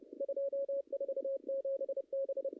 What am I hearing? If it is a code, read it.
Morse code, heard as a keyed tone, about 30 wpm, 540 Hz. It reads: 247 B
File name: contests / ss-cw-k5tr-op-n5rz -0000